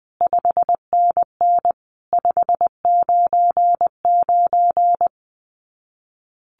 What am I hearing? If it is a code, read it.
Morse code, heard as a keyed tone, about 20 wpm, 700 Hz. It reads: 5NN 599
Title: Common QSO Phrases 20wpm